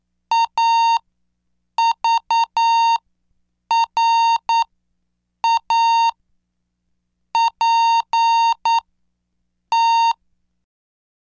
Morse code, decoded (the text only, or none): A V R A P T